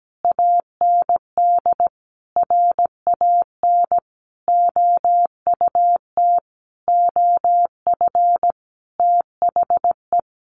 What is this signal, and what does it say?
Morse code, heard as a keyed tone, about 17 wpm, 700 Hz. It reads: AND RAN OUT OF THE
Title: Sentences from Top 500 Words 17wpm